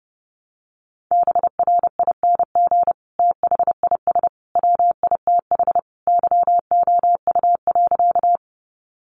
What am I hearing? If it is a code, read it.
Morse code, heard as a keyed tone, about 30 wpm, 700 Hz. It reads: BRING THIS WITH YOU.
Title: Sentences from Top 500 Words 30wpm